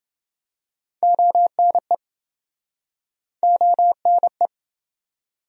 Morse code, heard as a keyed tone, about 28 wpm, 700 Hz. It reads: ONE ONE